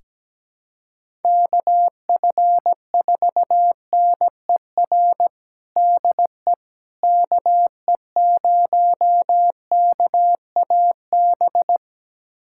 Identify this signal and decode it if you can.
Morse code, heard as a keyed tone, about 17 wpm, 700 Hz. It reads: KF4NER DE KE0KAB
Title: USA Callsign DE Callsign 17wpm